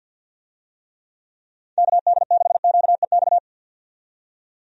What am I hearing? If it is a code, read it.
Morse code, heard as a keyed tone, about 53 wpm, 700 Hz. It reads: KD6QEX
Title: USA Call Signs - 6 Characters - 50wpm